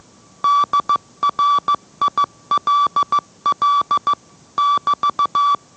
Morse code, heard as a keyed tone, about 15 wpm, 1160 Hz. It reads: DRILL <BT>